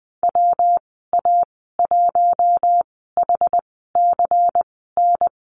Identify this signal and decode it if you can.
Morse code, heard as a keyed tone, about 20 wpm, spaced as for 15 wpm, 700 Hz. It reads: WA1HCN
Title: 128 Lesson 21g - Number 6 - Call Signs 20wpm